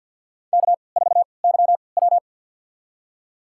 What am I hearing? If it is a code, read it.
Morse code, heard as a keyed tone, about 52 wpm, spaced as for 29 wpm, 700 Hz. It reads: K4YW
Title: USA Call Signs - 4 Characters - 3x Character Spacing - 50wpm